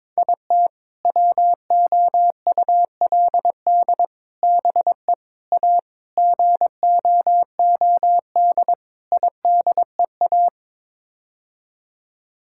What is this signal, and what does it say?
Morse code, heard as a keyed tone, about 22 wpm, 700 Hz. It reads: IT WOULD BE A GOOD IDEA
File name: practice / 22wpm / Sets of 6 Words - 22wpm